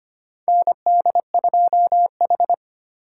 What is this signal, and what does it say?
Morse code, heard as a keyed tone, about 25 wpm, 700 Hz. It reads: ND2H
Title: USA Call Signs - 4 Characters - 8x Character Spacing - 25wpm